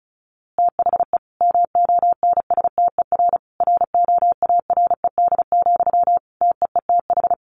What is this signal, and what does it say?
Morse code, heard as a keyed tone, about 34 wpm, 700 Hz. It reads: THE MONSTER ROARED, TEETH
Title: A Sound Of Thunder 35wpm